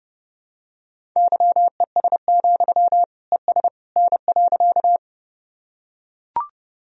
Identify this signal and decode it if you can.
Morse code, heard as a keyed tone, about 30 wpm, 700 Hz. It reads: YES, ES N.
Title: Sentences from Top 500 Words and Abbreviations 30wpm